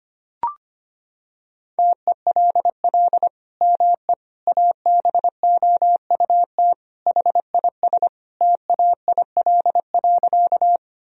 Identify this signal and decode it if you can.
Morse code, heard as a keyed tone, about 25 wpm, 700 Hz. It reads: TELL ME ABOUT HIS TAIL.
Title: Sentences from Top 1000 Words - Review Part 3 - 25wpm